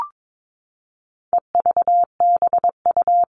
Morse code, heard as a keyed tone, about 22 wpm, 700 Hz. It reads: EVBU
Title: All Cut Numbers - 4-Digit 22wpm